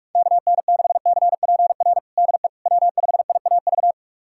Morse code, heard as a keyed tone, about 46 wpm, 700 Hz. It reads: KN6CPR DE W5IAV
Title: USA Callsign DE Callsign 45wpm